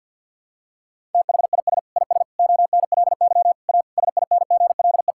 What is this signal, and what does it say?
Morse code, heard as a keyed tone, about 52 wpm, 700 Hz. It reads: THIS IS ONLY A SINGLE
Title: Sentences from Top 1000 Words - Review Part 5 - 50wpm